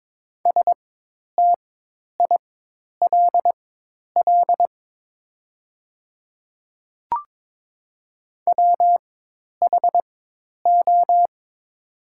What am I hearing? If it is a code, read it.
Morse code, heard as a keyed tone, about 22 wpm, spaced as for 10 wpm, 700 Hz. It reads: STILL WHO
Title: 4x Character Spacing - Top 300 Words - 22wpm